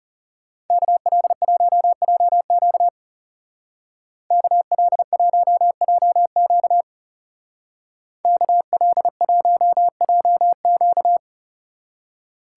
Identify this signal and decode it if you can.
Morse code, heard as a keyed tone, about 37 wpm, 700 Hz. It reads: KL1JQ KL1JQ KL1JQ
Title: Speed Racing - USA 2x2 Call Signs 30wpm